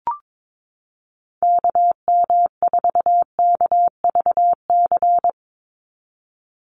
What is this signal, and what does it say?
Morse code, heard as a keyed tone, about 22 wpm, 700 Hz. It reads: KM4KVC
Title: USA Call Signs 22wpm